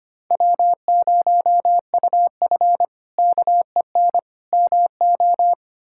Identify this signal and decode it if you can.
Morse code, heard as a keyed tone, about 25 wpm, 700 Hz. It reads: W0UF KEN MO
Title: K1USN SST QSO Practice - 25wpm